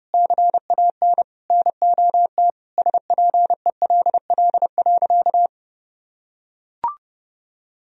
Morse code, heard as a keyed tone, about 30 wpm, 700 Hz. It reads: CAN NOT SPELL.
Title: Sentences from Top 1000 Words - Review Part 1 - 30wpm